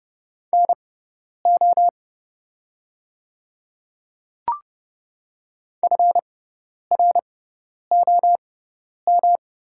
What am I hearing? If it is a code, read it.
Morse code, heard as a keyed tone, about 31 wpm, spaced as for 10 wpm, 700 Hz. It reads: NO FROM
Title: 6x Character Spacing - Top 100 Words - 30wpm